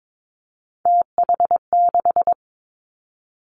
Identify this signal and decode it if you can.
Morse code, heard as a keyed tone, about 22 wpm, 700 Hz. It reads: TH6
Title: Three Challenging Letter-Number Combinations 22wpm